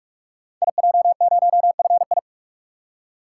Morse code, heard as a keyed tone, about 44 wpm, 700 Hz. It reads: IJ0FI
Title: Code Groups - 5 Letter:Numbers - 45wpm